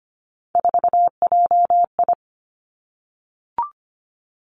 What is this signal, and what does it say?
Morse code, heard as a keyed tone, about 25 wpm, 700 Hz. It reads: 4JI
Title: Three Challenging Letter-Number Combinations 25wpm